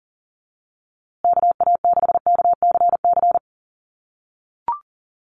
Morse code, heard as a keyed tone, about 38 wpm, 700 Hz. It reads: KA6KCC
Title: USA Call Signs Repeated 2x 40wpm